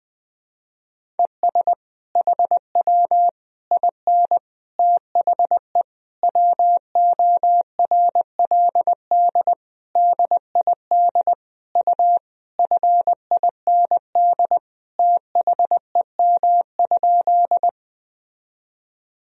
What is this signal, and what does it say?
Morse code, heard as a keyed tone, about 21 wpm, 700 Hz. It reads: ES HW IN THE WORLD DID U FIND THEM?
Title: Sentences from Top 200 Words and Abbreviations 20wpm